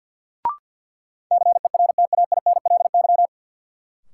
Morse code, heard as a keyed tone, about 53 wpm, 700 Hz. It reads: CERTAINLY
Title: Sentences from Top 900 Words - New Word Drill - 50wpm